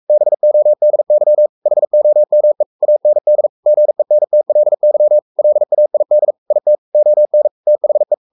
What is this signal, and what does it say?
Morse code, heard as a keyed tone, about 44 wpm, 600 Hz. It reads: BODY HOME AND GENTLY LAID IT ON THE
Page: Romanen Never af Ken Follet på engelsk.